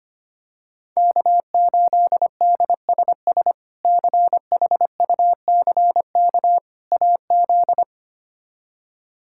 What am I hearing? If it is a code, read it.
Morse code, heard as a keyed tone, about 25 wpm, 700 Hz. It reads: K8DSS CHUCK AZ
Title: K1USN SST QSO Practice - 25wpm